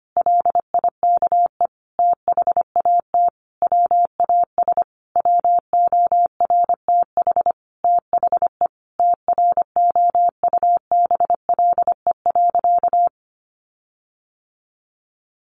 Morse code, heard as a keyed tone, about 25 wpm, 700 Hz. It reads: LIKE THAT WAS WORTH THE TROUBLE.
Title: Sentences from Top 1500 Words 25wpm